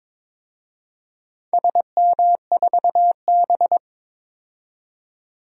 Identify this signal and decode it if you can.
Morse code, heard as a keyed tone, about 22 wpm, 700 Hz. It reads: SM4B
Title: DXCC-100 Call Signs - 4 Characters - 22wpm